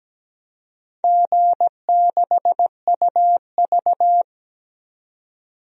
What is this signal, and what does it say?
Morse code, heard as a keyed tone, about 17 wpm, 700 Hz. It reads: G6UV
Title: DXCC-100 Call Signs - 4 Characters - 17wpm